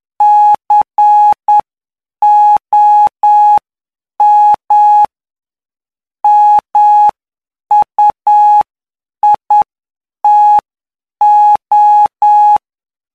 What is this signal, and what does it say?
Morse code, heard as a keyed tone, about 9 wpm, spaced as for 7 wpm, 840 Hz. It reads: COM MUITO
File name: cw / py2lcd / 18-PY2LCD-Li��o 27